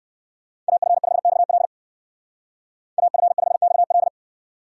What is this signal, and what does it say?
Morse code, heard as a keyed tone, about 59 wpm, 700 Hz. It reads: AF5XL AF5XL